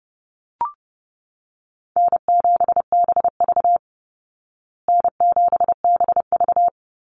Morse code, heard as a keyed tone, about 30 wpm, 700 Hz. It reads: N7BV N7BV
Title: Easy Call Signs Repeated 3x - 30wpm